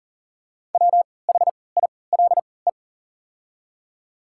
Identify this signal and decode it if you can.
Morse code, heard as a keyed tone, about 40 wpm, spaced as for 23 wpm, 700 Hz. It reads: WHILE
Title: 3x Character Spacing - Top 300 Words - 40wpm